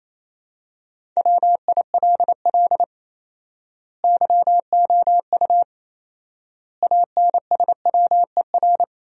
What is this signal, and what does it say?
Morse code, heard as a keyed tone, about 28 wpm, 700 Hz. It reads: WILL YOU ANSWER
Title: 4x Word-Spacing - Sentences from Top 400 Words - 28wpm